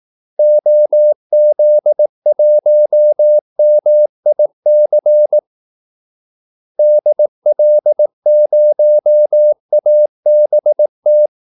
Morse code, heard as a keyed tone, about 18 wpm, 600 Hz. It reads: OZ1MIC DL0ABT
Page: Kaldesignaler 18 wpm